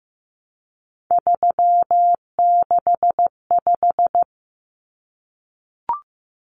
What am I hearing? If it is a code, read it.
Morse code, heard as a keyed tone, about 15 wpm, 700 Hz. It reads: 365
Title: Common Cut Numbers - 3-Digit 15wpm